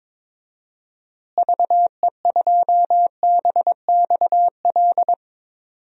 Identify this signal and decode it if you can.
Morse code, heard as a keyed tone, about 22 wpm, 700 Hz. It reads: VE2BXL
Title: DXCC-100 Call Signs - 6 Characters - Repeated 2x - 22wpm